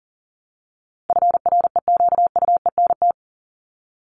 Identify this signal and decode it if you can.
Morse code, heard as a keyed tone, about 40 wpm, 700 Hz. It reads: FREQUENT
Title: LY Suffix - Top 100 Words with Base Words 40wpm